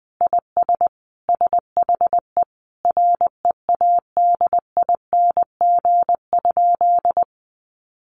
Morse code, heard as a keyed tone, about 20 wpm, 700 Hz. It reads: IS SHE READING?
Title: Sentences from Top 1200 Words 20wpm